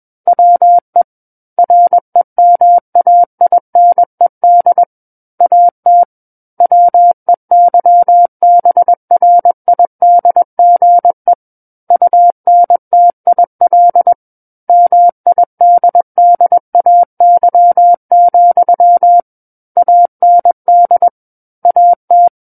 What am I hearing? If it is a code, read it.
Morse code, heard as a keyed tone, about 21 wpm, 700 Hz. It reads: WE REMAINED AT WEYBRIDGE UNTIL MIDDAY, AND AT
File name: Book One / 12-Chapter 12 - 21 WPM